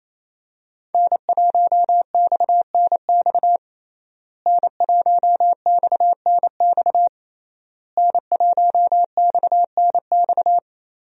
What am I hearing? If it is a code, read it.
Morse code, heard as a keyed tone, about 28 wpm, 700 Hz. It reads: N1XNX N1XNX N1XNX
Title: USA Call Signs - 5 Characters - Repeated 3x - 28wpm